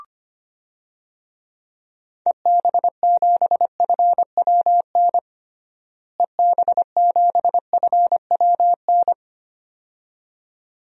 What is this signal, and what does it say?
Morse code, heard as a keyed tone, about 25 wpm, 700 Hz. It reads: EB7FWN EB7FWN
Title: DXCC-100 Call Signs - 6 Characters - Repeated 2x - 25wpm